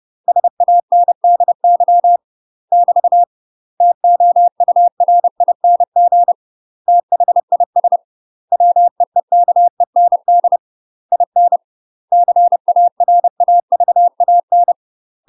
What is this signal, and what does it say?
Morse code, heard as a keyed tone, about 30 wpm, 700 Hz. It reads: SANDY <BT> TOURING THIS WEEKEND IN CARAVAN
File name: UK MASTER QSOs TRAINING / 30WPM / UK MASTER QSOs 0034